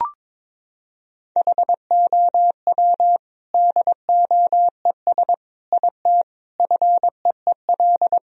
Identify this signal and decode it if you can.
Morse code, heard as a keyed tone, about 22 wpm, 700 Hz. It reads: HOW DOES IT FEEL
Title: Sentences from Top 1000 Words - Review Part 2 - 22wpm